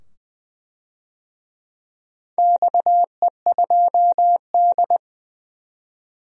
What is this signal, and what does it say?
Morse code, heard as a keyed tone, about 20 wpm, 700 Hz. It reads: XE2D
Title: DXCC-100 Call Signs - 4 Characters - 20wpm